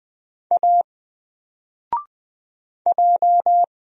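Morse code, heard as a keyed tone, about 20 wpm, 700 Hz. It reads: A J
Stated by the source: Single letter and number Morse drill at 20 wpm, from N4LCD.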